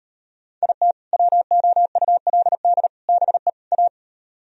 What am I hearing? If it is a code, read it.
Morse code, heard as a keyed tone, about 38 wpm, 700 Hz. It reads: IT WOULD BE A